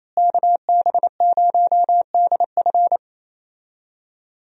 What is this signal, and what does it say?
Morse code, heard as a keyed tone, about 28 wpm, 700 Hz. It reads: KB0DF
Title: USA Call Signs - 5 Characters - Repeated 2x - 28wpm